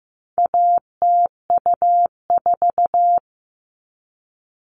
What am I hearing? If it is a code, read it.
Morse code, heard as a keyed tone, about 15 wpm, 700 Hz. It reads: ATU4
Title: All Cut Numbers - 4-Digit 15wpm